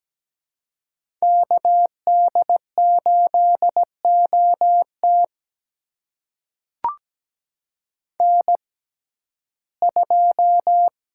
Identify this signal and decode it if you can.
Morse code, heard as a keyed tone, about 17 wpm, spaced as for 12 wpm, 700 Hz. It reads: KD8OT N2
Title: USA Call Signs - 5 Characters - 6x Character Spacing - 17wpm